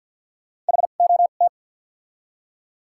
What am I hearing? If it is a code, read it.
Morse code, heard as a keyed tone, about 53 wpm, spaced as for 37 wpm, 700 Hz. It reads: HOT
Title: 2x Character Spacing - Top 300 Words - 50wpm